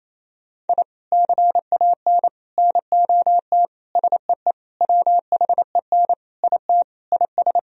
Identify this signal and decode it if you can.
Morse code, heard as a keyed tone, about 27 wpm, 700 Hz. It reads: I CAN NOT SEE WHEN IT IS
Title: Sentences from Top 1000 Words - Review Part 4 - 28wpm